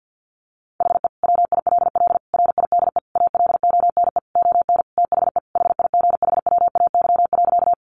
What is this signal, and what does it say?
Morse code, heard as a keyed tone, about 50 wpm, 700 Hz. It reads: HE WILL RIDE ALONE ON THE HIGHWAY.
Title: Sentences from Top 1400 Words 50wpm